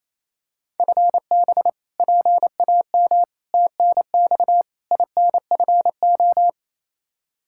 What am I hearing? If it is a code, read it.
Morse code, heard as a keyed tone, about 27 wpm, 700 Hz. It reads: FB PAM TNX INFO
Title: Practice QSO - Exchange 3 - More Info - 28wpm